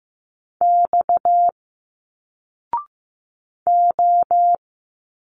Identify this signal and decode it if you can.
Morse code, heard as a keyed tone, about 15 wpm, 700 Hz. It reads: X O
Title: Single Letters 15wpm